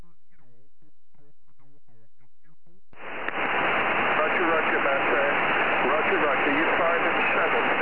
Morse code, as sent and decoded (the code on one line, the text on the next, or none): none